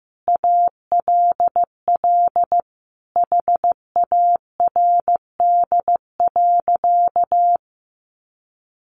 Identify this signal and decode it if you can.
Morse code, heard as a keyed tone, about 15 wpm, 700 Hz. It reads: ALL HARD.
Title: Additional Easy Phrases 15wpm